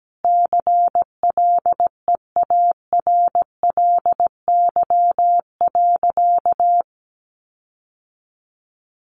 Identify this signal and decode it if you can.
Morse code, heard as a keyed tone, about 17 wpm, 700 Hz. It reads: CLEARLY.
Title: Sentences from Top 1300 Words 17wpm